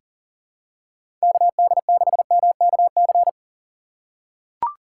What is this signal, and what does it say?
Morse code, heard as a keyed tone, about 40 wpm, 700 Hz. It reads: KD6MKC
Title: USA Call Signs - 6 Characters - 4x Character Spacing - 40wpm